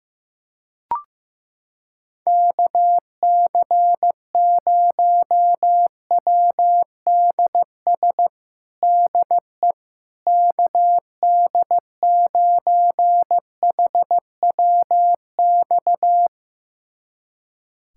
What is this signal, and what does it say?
Morse code, heard as a keyed tone, about 15 wpm, 700 Hz. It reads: KC0WDS DE KD9HWX
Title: USA Callsign DE Callsign - 6 Characters - 15wpm